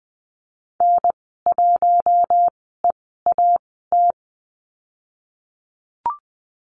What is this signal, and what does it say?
Morse code, heard as a keyed tone, about 20 wpm, spaced as for 15 wpm, 700 Hz. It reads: N1EAT
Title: 048 Lesson 9g - Number 4 - Call Signs 20wpm